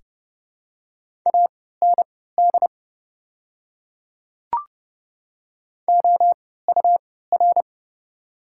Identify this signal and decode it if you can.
Morse code, heard as a keyed tone, about 31 wpm, spaced as for 17 wpm, 690 Hz. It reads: AND OUR
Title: 3x Character Spacing - Top 200 Words - 30wpm